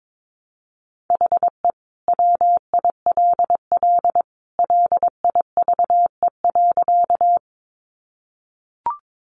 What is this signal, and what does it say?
Morse code, heard as a keyed tone, about 22 wpm, 700 Hz. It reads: HE WILL LIVE.
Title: Sentences from Top 200 Words 22wpm